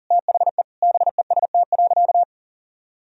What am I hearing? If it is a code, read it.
Morse code, heard as a keyed tone, about 40 wpm, 700 Hz. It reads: THE BEST.
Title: Sentences from Top 500 Words and Abbreviations 40wpm